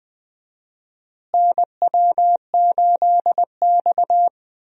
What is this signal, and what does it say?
Morse code, heard as a keyed tone, about 21 wpm, 700 Hz. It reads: NW8X
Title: USA Call Signs - 4 Characters - Repeated 3x - 20wpm